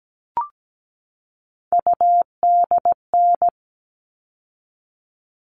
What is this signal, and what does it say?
Morse code, heard as a keyed tone, about 17 wpm, 700 Hz. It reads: UDN
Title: All Cut Numbers - 3-Digit 17wpm